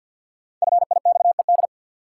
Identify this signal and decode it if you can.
Morse code, heard as a keyed tone, about 53 wpm, 700 Hz. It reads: FIXED
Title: Sentences from Top 1100 Words - New Word Drill Alt - 50wpm